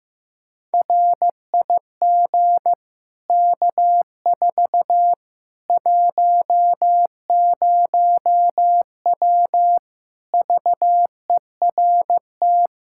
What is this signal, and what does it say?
Morse code, heard as a keyed tone, about 15 wpm, 700 Hz. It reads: RIG K4 10W VERT
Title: Practice QSO - Exchange 2 - Info - 15wpm